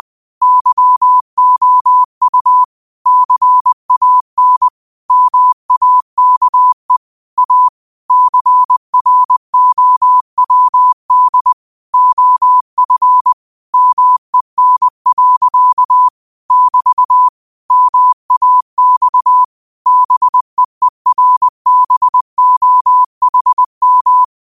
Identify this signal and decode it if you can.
Morse code, heard as a keyed tone, about 20 wpm, 1000 Hz. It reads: YOU CAN MAKE A CROWD OF MEN. <BT> MAX BEERBOHM
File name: qotdfiles / 20wpm